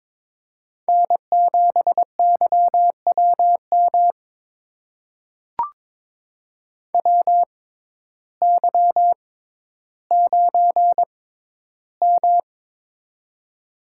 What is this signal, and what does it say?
Morse code, heard as a keyed tone, about 23 wpm, spaced as for 12 wpm, 700 Hz. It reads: N7YWM WY9M
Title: USA Call Signs - 6x Character Spacing - 22wpm